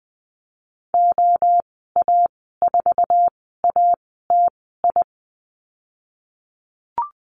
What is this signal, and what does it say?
Morse code, heard as a keyed tone, about 20 wpm, spaced as for 14 wpm, 700 Hz. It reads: OA4ATI
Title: 069 Lesson 12g - Letter d - Call Signs 20wpm